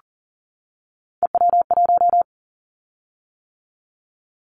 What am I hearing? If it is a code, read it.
Morse code, heard as a keyed tone, about 42 wpm, 700 Hz. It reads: EW1
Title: Callsign Prefixes for Top 50 DXCC - 40wpm